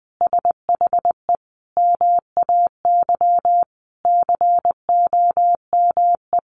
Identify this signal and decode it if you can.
Morse code, heard as a keyed tone, about 20 wpm, 700 Hz. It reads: SHE MAY COME
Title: Sentences from Top 1400 Words 20wpm